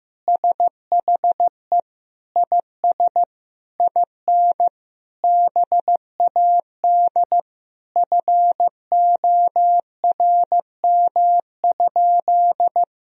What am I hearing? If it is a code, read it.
Morse code, heard as a keyed tone, about 15 wpm, 700 Hz. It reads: SHE IS IN BAD FORM?
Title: Sentences from Top 1000 Words - Review Part 1 - 15wpm